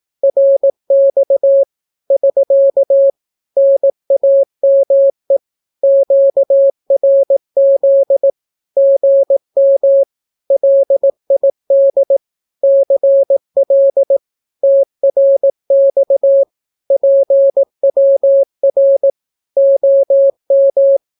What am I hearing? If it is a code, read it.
Morse code, heard as a keyed tone, about 18 wpm, 550 Hz. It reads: RX <SK> NAME QRZ GM LID CL TRX PWR OM
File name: naukapodstaw / skroty / skroty